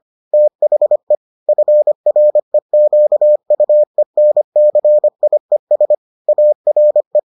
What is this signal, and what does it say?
Morse code, heard as a keyed tone, about 25 wpm, 600 Hz. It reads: THE FREQUENCIES ARE